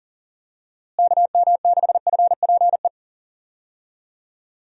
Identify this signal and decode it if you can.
Morse code, heard as a keyed tone, about 40 wpm, 700 Hz. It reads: KM6FPE
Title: USA Call Signs - 6 Characters - Repeated 2x - 40wpm